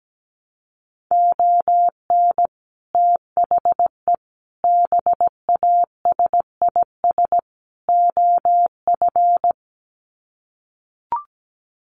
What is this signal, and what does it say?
Morse code, heard as a keyed tone, about 17 wpm, 700 Hz. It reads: ON THE BASIS OF
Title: Sets of 4 Words 17wpm